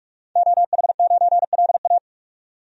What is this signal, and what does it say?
Morse code, heard as a keyed tone, about 41 wpm, 700 Hz. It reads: OH9LA
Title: DXCC-100 Call Signs - 5 Characters - 45wpm